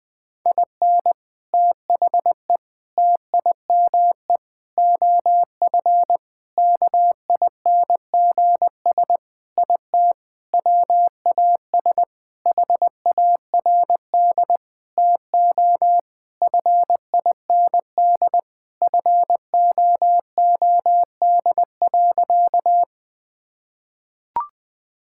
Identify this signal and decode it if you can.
Morse code, heard as a keyed tone, about 20 wpm, 700 Hz. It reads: IN THE TIME OF KINGS IT WAS HARD TO FIND FOOD.
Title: Sentences with 3-Word Phrases - 20wpm